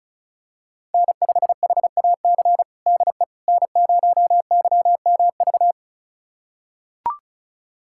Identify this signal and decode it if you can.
Morse code, heard as a keyed tone, about 36 wpm, 690 Hz. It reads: N5HAC DE N0YMV
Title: USA Callsign DE Callsign - 5 Characters - 35wpm